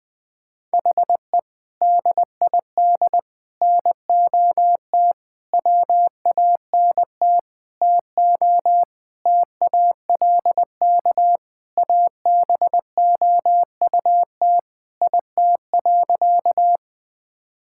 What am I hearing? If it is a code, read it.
Morse code, heard as a keyed tone, about 20 wpm, 700 Hz. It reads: HE DID NOT WANT TO TALK ABOUT IT.
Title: Sentences with 5-Word Phrases - 20wpm